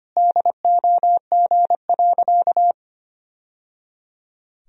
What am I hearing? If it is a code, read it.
Morse code, heard as a keyed tone, about 25 wpm, 700 Hz. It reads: DOG.
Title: Sentences from Top 1000 Words - Review Part 2 - 25wpm